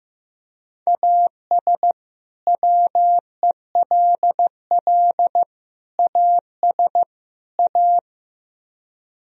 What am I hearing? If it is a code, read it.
Morse code, heard as a keyed tone, about 15 wpm, 700 Hz. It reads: AS WELL AS A
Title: Speed-Racing - Sets of 4 Words 15wpm